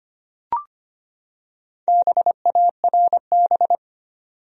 Morse code, heard as a keyed tone, about 25 wpm, 700 Hz. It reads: BARB
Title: USA Common Names - Max 5 Letters - 25wpm